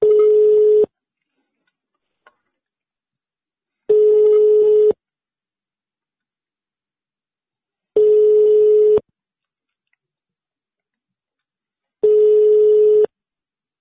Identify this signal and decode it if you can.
Ringback tone (425 Hz, 1 s on, 3 s off).